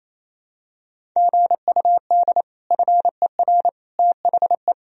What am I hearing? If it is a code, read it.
Morse code, heard as a keyed tone, about 28 wpm, 700 Hz. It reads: GUD FER THE
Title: Sentences from Top 300 Words and Abbreviations 28wpm